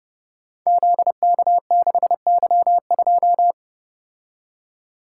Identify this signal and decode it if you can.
Morse code, heard as a keyed tone, about 30 wpm, 700 Hz. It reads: ZK6Y2
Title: Code Groups - 5 Letter:Numbers - 30wpm